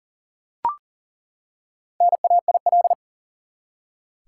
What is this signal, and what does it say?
Morse code, heard as a keyed tone, about 40 wpm, 700 Hz. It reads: NAIL
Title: Four Letter Words - 40wpm